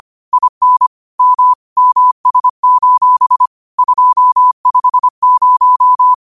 Morse code, heard as a keyed tone, about 25 wpm, 1000 Hz. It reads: IN M MS: 250